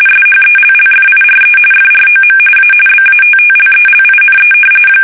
RTTY, about 45.45 baud. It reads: SHELLFIRE FROM A BATTLELINE
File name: fsk850 / battleshipnj